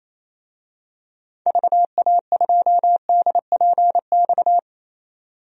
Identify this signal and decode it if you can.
Morse code, heard as a keyed tone, about 28 wpm, 700 Hz. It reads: VA2DPX
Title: DXCC-100 Call Signs - 6 Characters - 28wpm